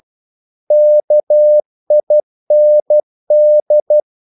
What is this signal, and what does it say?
Morse code, heard as a keyed tone, about 12 wpm, 600 Hz. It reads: KIND
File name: Korte ord DK 12wpm